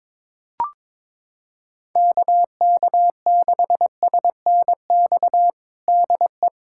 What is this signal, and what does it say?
Morse code, heard as a keyed tone, about 22 wpm, 700 Hz. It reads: KK6SNX DE
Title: USA Callsign DE Callsign - 6 Characters - 22wpm